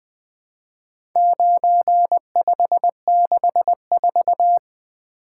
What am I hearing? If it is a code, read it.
Morse code, heard as a keyed tone, about 20 wpm, 700 Hz. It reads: 9564
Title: Four Letter-Number Combinations - 20wpm